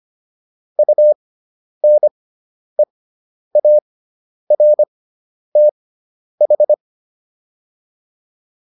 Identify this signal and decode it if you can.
Morse code, heard as a keyed tone, about 25 wpm, spaced as for 10 wpm, 600 Hz. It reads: UNEARTH